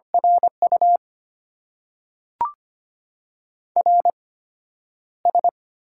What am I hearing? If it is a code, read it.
Morse code, heard as a keyed tone, about 25 wpm, spaced as for 11 wpm, 700 Hz. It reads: RU RS
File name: Top 90 QSO Elements - 8x Character Spacing - 25wpm